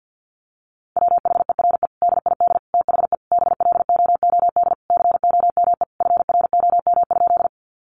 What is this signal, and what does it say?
Morse code, heard as a keyed tone, about 50 wpm, 700 Hz. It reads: WHERE DID THE BLOOD COME FROM?
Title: Sentences from Top 1100 Words 50wpm